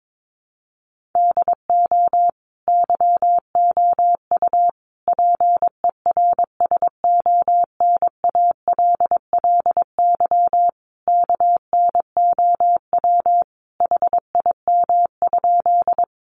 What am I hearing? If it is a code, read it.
Morse code, heard as a keyed tone, about 22 wpm, 700 Hz. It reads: DO YOU PERSONALLY KNOW HIM?
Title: Sentences from Top 1400 Words 22wpm